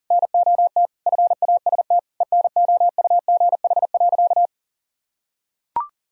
Morse code, heard as a keyed tone, about 40 wpm, 700 Hz. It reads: NOT FAST ENOUGH.
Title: Sentences from Top 1000 Words - Review Part 3 - 40wpm